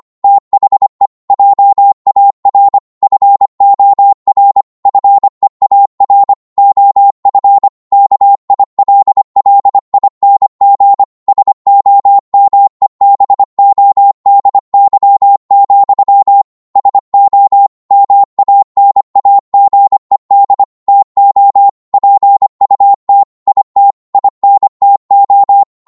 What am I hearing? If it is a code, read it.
Morse code, heard as a keyed tone, about 25 wpm, 800 Hz. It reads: THE JAR FOR FEAR OF KILLING SOMEBODY, SO MANAGED TO PUT IT INTO